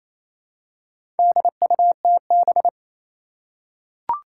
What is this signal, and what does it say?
Morse code, heard as a keyed tone, about 28 wpm, 700 Hz. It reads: DUTB
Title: Four Challenging Letter-Number Combinations 28wpm